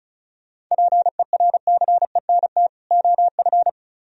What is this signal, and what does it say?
Morse code, heard as a keyed tone, about 33 wpm, 700 Hz. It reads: PERCENT OF
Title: Speed-Racing - Sets of 2 Words 35wpm